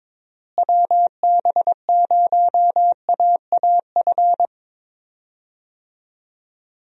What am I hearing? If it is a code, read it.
Morse code, heard as a keyed tone, about 22 wpm, 700 Hz. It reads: WB0AAF
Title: USA Call Signs - 6 Characters - Repeated 3x - 22wpm